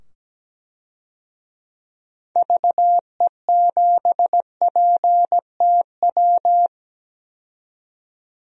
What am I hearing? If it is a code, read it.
Morse code, heard as a keyed tone, about 17 wpm, 700 Hz. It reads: VE7PTW
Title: DXCC-100 Call Signs - 6 Characters - 17wpm